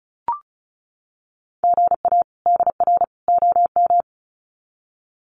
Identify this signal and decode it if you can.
Morse code, heard as a keyed tone, about 34 wpm, 700 Hz. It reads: GA DR OM
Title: Common QSO Phrases 35wpm